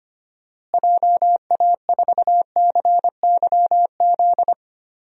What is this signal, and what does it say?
Morse code, heard as a keyed tone, about 25 wpm, 700 Hz. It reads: JA4CYZ
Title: DXCC-100 Call Signs - 6 Characters - Repeated 3x - 25wpm